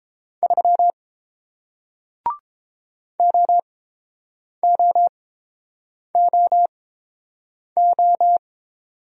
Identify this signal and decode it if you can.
Morse code, heard as a keyed tone, about 30 wpm, spaced as for 7 wpm, 700 Hz. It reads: 3 OOOO